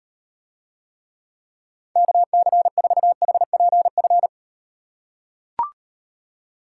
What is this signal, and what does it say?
Morse code, heard as a keyed tone, about 38 wpm, 700 Hz. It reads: KC4HPF